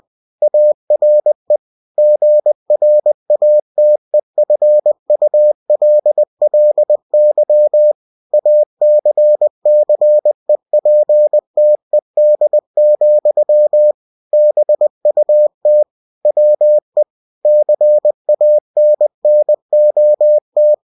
Morse code, heard as a keyed tone, about 20 wpm, 600 Hz. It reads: ARE GRATEFULLY ACCEPTED, BUT WE CANNOT